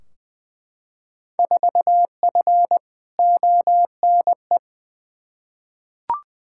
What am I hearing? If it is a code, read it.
Morse code, heard as a keyed tone, about 20 wpm, 700 Hz. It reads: 4F ONE
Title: Field Day Exchanges Encore - 20wpm